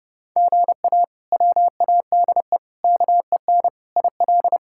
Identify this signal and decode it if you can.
Morse code, heard as a keyed tone, about 30 wpm, 700 Hz. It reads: GA WADE KEN IL
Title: K1USN SST QSO Practice - 30wpm